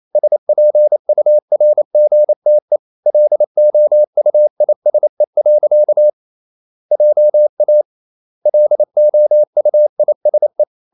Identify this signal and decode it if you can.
Morse code, heard as a keyed tone, about 28 wpm, 600 Hz. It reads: SPURGTE LOUISE. JA LOUISE
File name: Ferie QRQ 28wpm